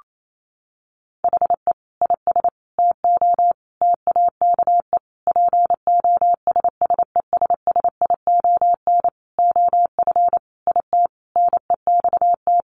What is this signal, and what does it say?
Morse code, heard as a keyed tone, about 28 wpm, 700 Hz. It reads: HE IS TO TAKE POSSESSION OF IT NEXT
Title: Sentences from Top 1400 Words 28wpm